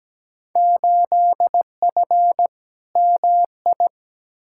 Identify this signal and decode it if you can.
Morse code, heard as a keyed tone, about 17 wpm, 700 Hz. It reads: 8F MI
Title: Field Day Exchanges Encore - 17wpm